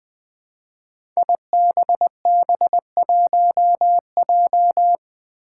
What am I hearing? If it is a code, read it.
Morse code, heard as a keyed tone, about 20 wpm, 700 Hz. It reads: IBB1J
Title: Five Challenging Letter-Number Combinations 20wpm